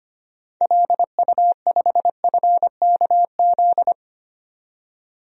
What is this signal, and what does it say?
Morse code, heard as a keyed tone, about 25 wpm, 700 Hz. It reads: LU5FKZ
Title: DXCC-100 Call Signs - 6 Characters - 25wpm